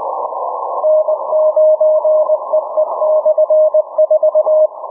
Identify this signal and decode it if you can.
Morse code, heard as a keyed tone, about 19 wpm, 620 Hz. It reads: N9EE/4
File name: cw / ssn 40M